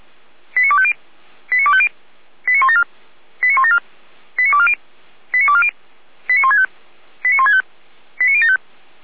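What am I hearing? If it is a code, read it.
Five-tone selective call (ZVEI), heard as quick runs of five tones, tones of about 70 ms. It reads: 89250 89250 89174 89174 89240 89240 89165 89165 89075
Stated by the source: ZVEI five-tone call 89337 received on an SDR by feuerwehr-orlamuende.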